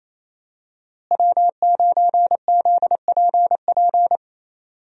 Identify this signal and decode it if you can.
Morse code, heard as a keyed tone, about 28 wpm, 700 Hz. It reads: W9ZPP
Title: USA Call Signs - 5 Characters - Repeated 3x - 28wpm